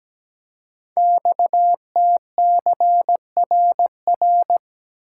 Morse code, heard as a keyed tone, about 17 wpm, 700 Hz. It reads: XTCRR
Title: Code Groups - 5 Letters - 17wpm